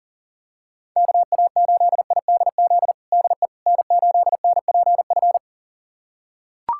Morse code, heard as a keyed tone, about 37 wpm, 700 Hz. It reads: KA8IDZ DE N8NPF
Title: USA Callsign DE Callsign 40wpm